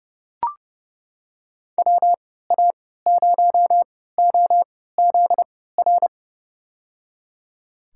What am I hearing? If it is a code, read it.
Morse code, heard as a keyed tone, about 30 wpm, spaced as for 17 wpm, 700 Hz. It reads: WA0OZR
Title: USA Call Signs - 6 Characters - 3x Character Spacing - 30wpm